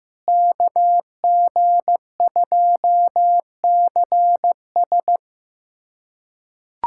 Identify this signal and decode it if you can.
Morse code, heard as a keyed tone, about 15 wpm, 700 Hz. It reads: KG2CS
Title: USA Callsign DE Callsign - 5 Characters - 15wpm